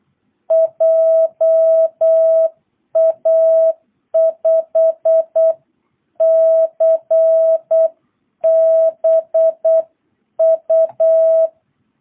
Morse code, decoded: JA5CBU

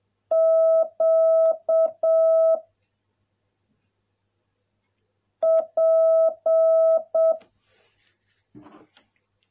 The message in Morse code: Q P

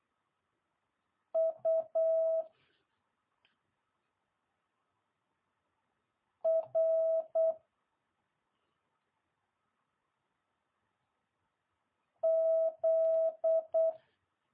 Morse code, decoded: U R Z

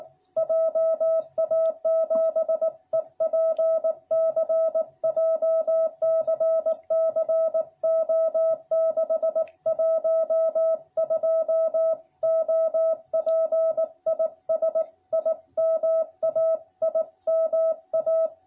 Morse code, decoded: JA7EPC JCC O612 OP IS IMA IMA

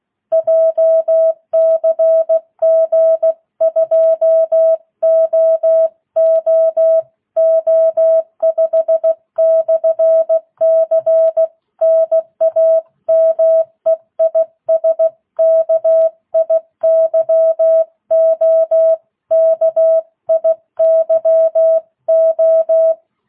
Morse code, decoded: JCG 2OOO5/C NAME IS KIYO KIYO

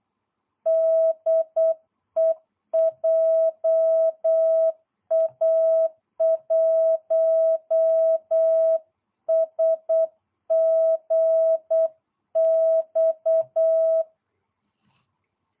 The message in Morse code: DE JA1SGX

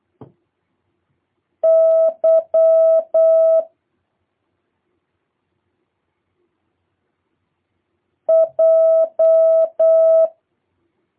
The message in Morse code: Y J

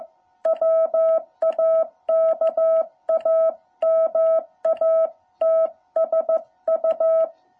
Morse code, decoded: WAKAMATSU

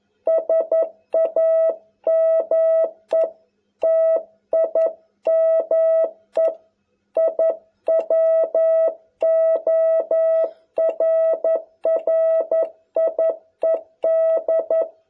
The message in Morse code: SAME TIME I WORRIED

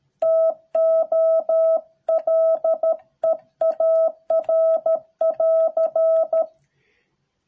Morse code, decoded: TO LEAR<AR>

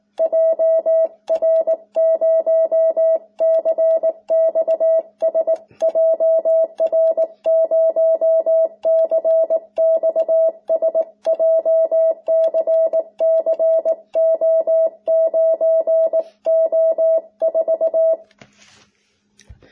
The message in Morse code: JR0CXS JR0CXS JCCO9O4

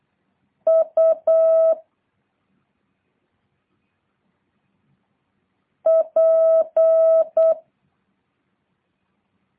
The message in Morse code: U P

